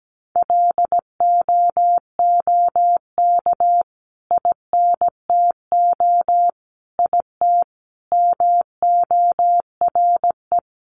Morse code, heard as a keyed tone, about 17 wpm, 700 Hz. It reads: LOOK INTO IT MORE
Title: Sentences from Top 1300 Words 17wpm